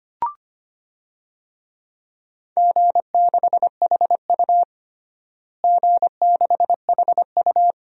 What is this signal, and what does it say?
Morse code, heard as a keyed tone, about 25 wpm, 700 Hz. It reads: G6HU G6HU
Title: DXCC-100 Call Signs - 4 Characters - Repeated 2x - 25wpm